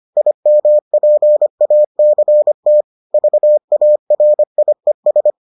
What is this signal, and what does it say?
Morse code, heard as a keyed tone, about 25 wpm, 600 Hz. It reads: IMPACT VARIES